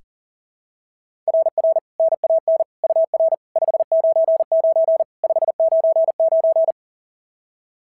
Morse code, heard as a keyed tone, about 37 wpm, 690 Hz. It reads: RR NAN UR 599 599
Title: SOTA QSO Practice 40wpm